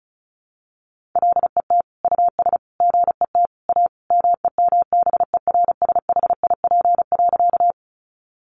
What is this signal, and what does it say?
Morse code, heard as a keyed tone, about 34 wpm, 700 Hz. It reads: LET US GET A MEMBERSHIP.
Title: Sentences from Top 1400 Words 35wpm